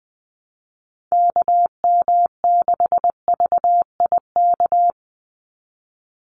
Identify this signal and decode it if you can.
Morse code, heard as a keyed tone, about 20 wpm, 700 Hz. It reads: KM6VIK
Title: USA Call Signs 20wpm